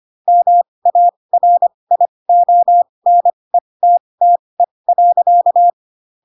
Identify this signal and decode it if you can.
Morse code, heard as a keyed tone, about 25 wpm, spaced as for 20 wpm, 700 Hz. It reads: MARIONETTE.
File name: Pinocchio / Pinocchio ch3 15 25 ik1hge